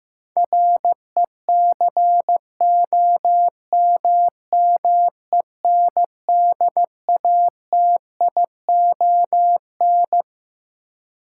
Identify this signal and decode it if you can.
Morse code, heard as a keyed tone, about 15 wpm, 700 Hz. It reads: RECOMMENDATION
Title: Sentences from Top 1400 Words - New Word Drill - 15wpm